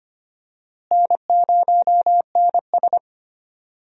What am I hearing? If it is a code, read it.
Morse code, heard as a keyed tone, about 25 wpm, 700 Hz. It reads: N0NS
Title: USA Call Signs - 4 Characters - Repeated 2x - 25wpm